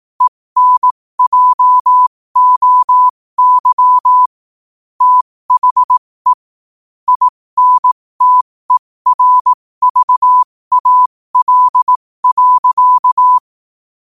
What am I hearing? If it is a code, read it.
Morse code, heard as a keyed tone, about 18 wpm, spaced as for 15 wpm, 1000 Hz. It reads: ENJOY THE INTERVAL.
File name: qotdfiles / 15wpm